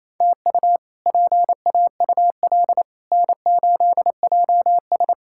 Morse code, heard as a keyed tone, about 28 wpm, 700 Hz. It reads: TU PAUL N8JS
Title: K1USN SST QSO Practice - 28wpm